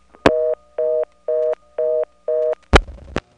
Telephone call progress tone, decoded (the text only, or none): reorder tone